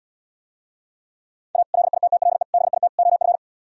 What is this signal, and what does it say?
Morse code, heard as a keyed tone, about 77 wpm, 700 Hz. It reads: A LITTLE BIT OF